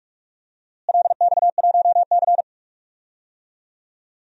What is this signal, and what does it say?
Morse code, heard as a keyed tone, about 46 wpm, 700 Hz. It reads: LX1C
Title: DXCC-100 Call Signs - 4 Characters - Repeated 3x - 45wpm